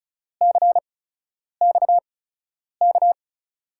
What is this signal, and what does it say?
Morse code, heard as a keyed tone, about 36 wpm, spaced as for 10 wpm, 700 Hz. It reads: CXK
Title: USA Call Signs - 6 Characters - 8x Character Spacing - 35wpm